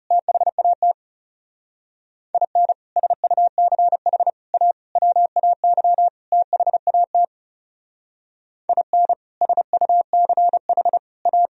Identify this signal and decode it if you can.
Morse code, heard as a keyed tone, about 36 wpm, 700 Hz. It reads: THAT IN SUCH A WAY THAT IN SUCH A
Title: Speed-Racing - Sets of 5 Words 30wpm